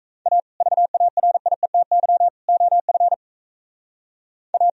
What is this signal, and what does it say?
Morse code, heard as a keyed tone, about 34 wpm, 700 Hz. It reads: A VARIETY OF A
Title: Speed-Racing - Sets of 3 Words 28wpm